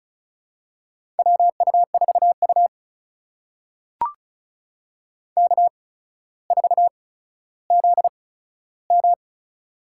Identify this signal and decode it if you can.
Morse code, heard as a keyed tone, about 34 wpm, spaced as for 15 wpm, 700 Hz. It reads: WU4U K4ZM
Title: USA Call Signs - 8x Character Spacing - 35wpm